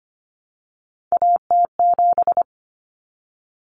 Morse code, heard as a keyed tone, about 25 wpm, 700 Hz. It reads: AT7
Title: Common Cut Numbers - 3-Digit 25wpm